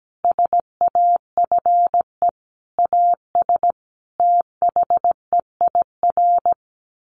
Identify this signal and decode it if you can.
Morse code, heard as a keyed tone, about 17 wpm, 700 Hz. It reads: SAFE AS THEIR
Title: Sentences from Top 1500 Words 17wpm